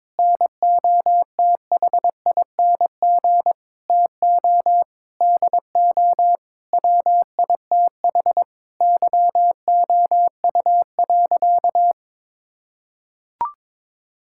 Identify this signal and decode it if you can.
Morse code, heard as a keyed tone, about 22 wpm, 700 Hz. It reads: NOTHING TO DO WITH YOU.
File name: practice / 22wpm / Sentences with 4-Word Phrases - 22wpm